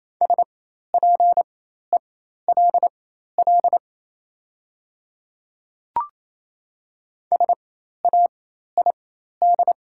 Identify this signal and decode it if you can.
Morse code, heard as a keyed tone, about 27 wpm, spaced as for 13 wpm, 700 Hz. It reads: SPELL SAID
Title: 4x Character Spacing - Top 200 Words - 28wpm